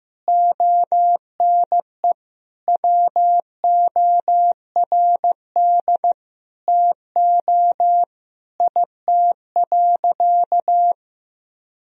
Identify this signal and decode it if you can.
Morse code, heard as a keyed tone, about 15 wpm, 700 Hz. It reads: ONE WORD TO IT.
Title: Sentences from Top 1000 Words - Review Part 1 - 15wpm